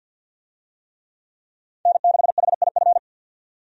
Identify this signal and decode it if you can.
Morse code, heard as a keyed tone, about 46 wpm, 700 Hz. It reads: N6HIF
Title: USA Call Signs - 5 Characters - 50wpm